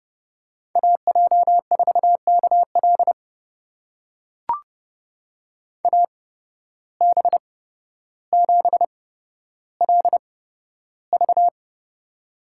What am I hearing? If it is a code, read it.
Morse code, heard as a keyed tone, about 31 wpm, spaced as for 13 wpm, 700 Hz. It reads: AJ4KL AB7LV
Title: USA Call Signs - 5 Characters - 8x Character Spacing - 30wpm